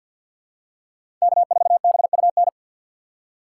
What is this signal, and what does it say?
Morse code, heard as a keyed tone, about 52 wpm, 700 Hz. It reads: K4BUN
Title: USA Call Signs - 5 Characters - 2x Character Spacing - 50wpm